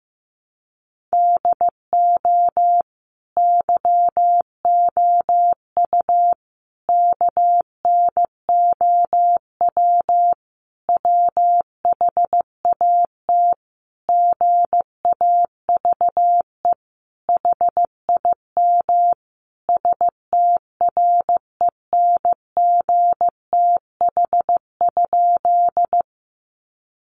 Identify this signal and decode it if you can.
Morse code, heard as a keyed tone, about 15 wpm, 700 Hz. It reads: DO YOU KNOW WHAT GAVE HIM STRENGTH?
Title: Sentences from Top 1300 Words 15wpm